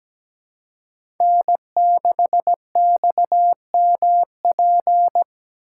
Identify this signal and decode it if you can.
Morse code, heard as a keyed tone, about 17 wpm, 700 Hz. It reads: N6XMP
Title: USA Call Signs - 3x Character Spacing - 17wpm